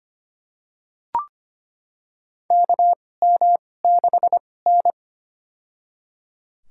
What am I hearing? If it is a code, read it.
Morse code, heard as a keyed tone, about 25 wpm, spaced as for 18 wpm, 700 Hz. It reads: KM6N
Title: USA Call Signs - 4 Characters - 2x Character Spacing - 25wpm